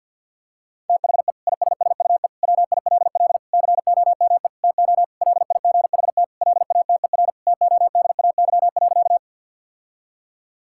Morse code, heard as a keyed tone, about 50 wpm, 700 Hz. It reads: THE ISSUE WILL COME TO LIGHT LATER TODAY.
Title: Sentences from Top 1000 Words - Review Part 6 - 50wpm